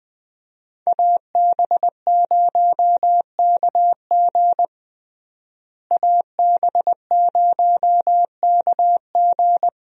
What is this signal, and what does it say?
Morse code, heard as a keyed tone, about 21 wpm, 700 Hz. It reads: AB0KG AB0KG
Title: USA Call Signs - 5 Characters - Repeated 3x - 20wpm